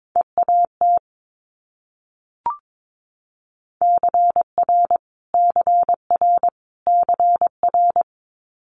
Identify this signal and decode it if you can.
Morse code, heard as a keyed tone, about 22 wpm, 700 Hz. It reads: EAT CR CR CR
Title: Multi-letter Phonemes Repeated 3x 22wpm